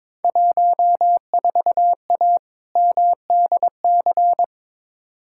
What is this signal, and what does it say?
Morse code, heard as a keyed tone, about 22 wpm, 700 Hz. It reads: 14A MDC
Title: Field Day Exchanges Encore - 22wpm